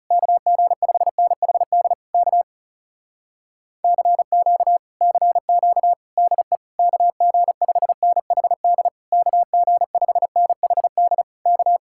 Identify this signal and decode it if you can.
Morse code, heard as a keyed tone, about 40 wpm, 700 Hz. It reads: KG5NHD K CQ CQ DE KG5NHD KG5NHD K